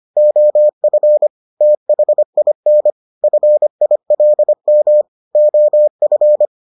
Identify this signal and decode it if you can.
Morse code, heard as a keyed tone, about 25 wpm, 600 Hz. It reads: OF THIN FILM OF